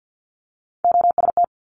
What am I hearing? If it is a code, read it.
Morse code, heard as a keyed tone, about 50 wpm, 700 Hz. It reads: OST
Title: Multi-letter Phonemes Quick Fire 50wpm